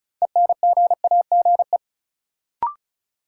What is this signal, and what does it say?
Morse code, heard as a keyed tone, about 34 wpm, 700 Hz. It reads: ENGAGE
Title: Sentences from Top 1500 Words - New Word Drill Alt - 35wpm